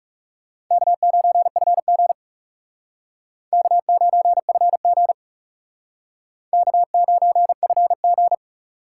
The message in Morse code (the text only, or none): K9FG K9FG K9FG